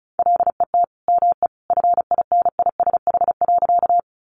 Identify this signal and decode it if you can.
Morse code, heard as a keyed tone, about 34 wpm, 700 Hz. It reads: LET ME FINISH.
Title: Sentences from Top 1200 Words 35wpm